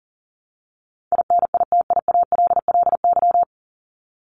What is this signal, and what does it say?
Morse code, heard as a keyed tone, about 38 wpm, 700 Hz. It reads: INITIALLY
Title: Top 1251-1500 Words 40wpm